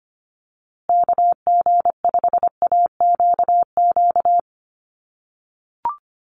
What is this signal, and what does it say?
Morse code, heard as a keyed tone, about 25 wpm, 700 Hz. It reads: KG5AQQ
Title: USA Call Signs Repeated 3x 25wpm